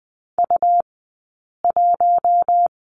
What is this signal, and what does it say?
Morse code, heard as a keyed tone, about 20 wpm, spaced as for 8 wpm, 700 Hz. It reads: U1
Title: 151 Lesson 25d - Letter p - Two Characters 20wpm